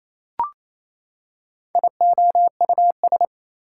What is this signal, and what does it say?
Morse code, heard as a keyed tone, about 27 wpm, 700 Hz. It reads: IOUS
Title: Prefixes and Suffixes 28wpm